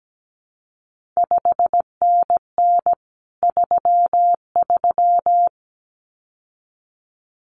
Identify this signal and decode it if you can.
Morse code, heard as a keyed tone, about 17 wpm, 700 Hz. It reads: 5NN 33
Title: CQ WW DX Contest Exchange Encore - 17wpm